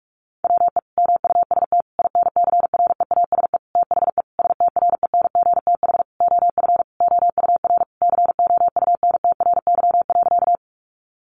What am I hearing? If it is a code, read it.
Morse code, heard as a keyed tone, about 44 wpm, 700 Hz. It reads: WE MUST INCREASE THE STRENGTH OF OUR COUNTRY.
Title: Sentences from Top 1300 Words 45wpm